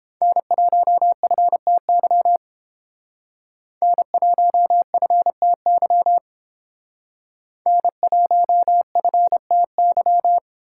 Morse code, heard as a keyed tone, about 30 wpm, 700 Hz. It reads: N1FTY N1FTY N1FTY